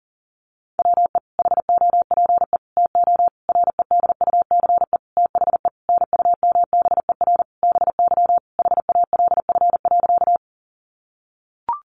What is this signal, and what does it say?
Morse code, heard as a keyed tone, about 37 wpm, 700 Hz. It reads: WE HOPE TO REDUCE THE NUMBER BY HALF.
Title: Sentences from Top 1100 Words 40wpm